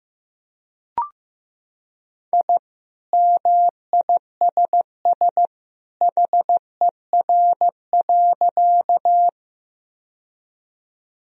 Morse code, heard as a keyed tone, about 15 wpm, 700 Hz. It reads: I MISS HER.
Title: Sentences from Top 1000 Words - Review Part 3 - 15wpm